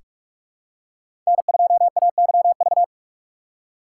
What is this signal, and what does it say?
Morse code, heard as a keyed tone, about 46 wpm, 700 Hz. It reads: N2AYV
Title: USA Call Signs - 5 Characters - 3x Character Spacing - 45wpm